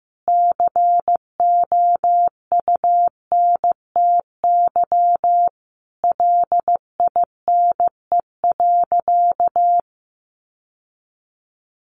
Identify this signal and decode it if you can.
Morse code, heard as a keyed tone, about 15 wpm, 700 Hz. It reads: COUNTY LINE.
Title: Sentences from Top 1200 Words 15wpm